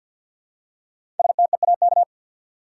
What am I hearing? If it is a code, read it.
Morse code, heard as a keyed tone, about 48 wpm, 700 Hz. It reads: STEAK
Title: Five Letter Words - 50wpm